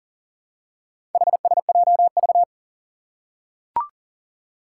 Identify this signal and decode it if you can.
Morse code, heard as a keyed tone, about 38 wpm, 700 Hz. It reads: HSJV
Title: Four Challenging Letter-Number Combinations 40wpm